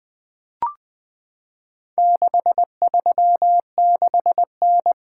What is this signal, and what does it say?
Morse code, heard as a keyed tone, about 20 wpm, 700 Hz. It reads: 636N
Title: Four Letter-Number Combinations - 20wpm